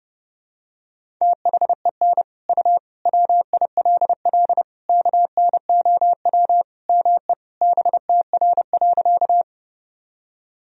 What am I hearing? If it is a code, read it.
Morse code, heard as a keyed tone, about 28 wpm, 700 Hz. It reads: THEN U WILL KNOW ME BTR.
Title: Sentences from Top 400 Words and Abbreviations 30wpm